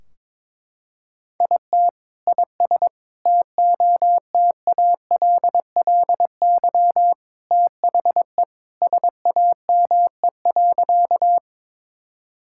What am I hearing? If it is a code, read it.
Morse code, heard as a keyed tone, about 22 wpm, 700 Hz. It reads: IT IS TOTALLY THE SAME.
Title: Sentences from Top 1600 Words 22wpm